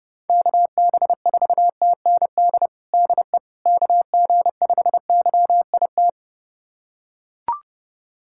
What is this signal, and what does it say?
Morse code, heard as a keyed tone, about 30 wpm, 700 Hz. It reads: KB4TND DE KG5YIT
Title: USA Callsign DE Callsign - 6 Characters - 30wpm